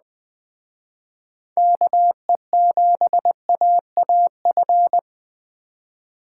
Morse code, heard as a keyed tone, about 21 wpm, 700 Hz. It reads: KE7AAF
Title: Speed Racing - USA 2x3 Call Signs 17wpm